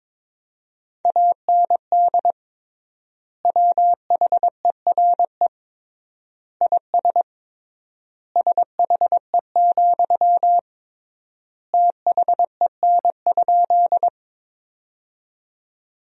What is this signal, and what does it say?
Morse code, heard as a keyed tone, about 22 wpm, 700 Hz. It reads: AND WHERE IS SHE, THEN?
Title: 3x Word-Spacing - Sentences from Top 200 Words - 22wpm